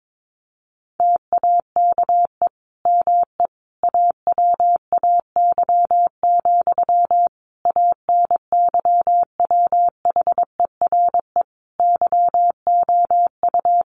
Morse code, heard as a keyed tone, about 22 wpm, 700 Hz. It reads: TAKE ME AWAY, ANYWHERE YOU
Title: Sentences from Top 1400 Words 22wpm